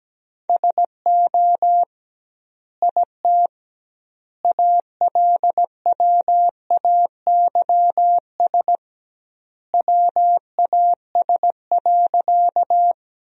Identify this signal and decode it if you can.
Morse code, heard as a keyed tone, about 17 wpm, 700 Hz. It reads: SO IT ALWAYS WAS.
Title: 2x Word-Spacing - Sentences from Top 300 Words - 17wpm